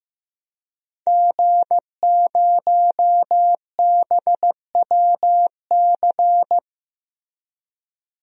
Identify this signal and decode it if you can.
Morse code, heard as a keyed tone, about 15 wpm, 700 Hz. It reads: G0BWC
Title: DXCC-100 Call Signs - 5 Characters - Repeated 3x - 15wpm